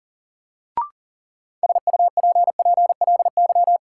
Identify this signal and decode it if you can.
Morse code, heard as a keyed tone, about 40 wpm, 700 Hz. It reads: SUPPLY
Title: Sentences from Top 1200 Words - New Word Drill Alt - 40wpm